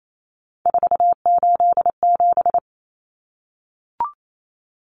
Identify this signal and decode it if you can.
Morse code, heard as a keyed tone, about 28 wpm, 700 Hz. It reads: 487
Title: Common Cut Numbers - 3-Digit 28wpm